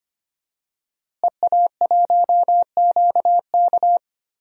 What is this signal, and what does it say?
Morse code, heard as a keyed tone, about 25 wpm, 700 Hz. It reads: EA1QK
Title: DXCC-100 Call Signs - 5 Characters - Repeated 3x - 25wpm